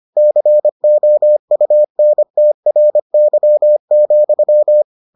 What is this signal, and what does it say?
Morse code, heard as a keyed tone, about 25 wpm, 600 Hz. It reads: COUNTRY,